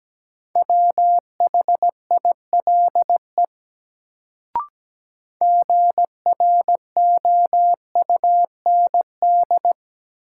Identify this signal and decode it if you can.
Morse code, heard as a keyed tone, about 17 wpm, 700 Hz. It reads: WHILE GROUND
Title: Sets of 3 Words - New Word Drill - 17wpm